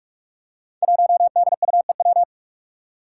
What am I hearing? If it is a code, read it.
Morse code, heard as a keyed tone, about 44 wpm, 700 Hz. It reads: 1DUEW
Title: Five Challenging Letter-Number Combinations 45wpm